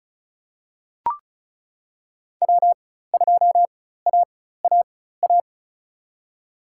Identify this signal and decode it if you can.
Morse code, heard as a keyed tone, about 36 wpm, spaced as for 16 wpm, 700 Hz. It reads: W2AAA
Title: USA Call Signs - 4x Character Spacing - 35wpm